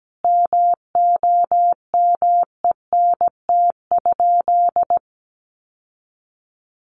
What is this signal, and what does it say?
Morse code, heard as a keyed tone, about 17 wpm, 700 Hz. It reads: MOMENT?
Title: Sentences from Top 1000 Words 17wpm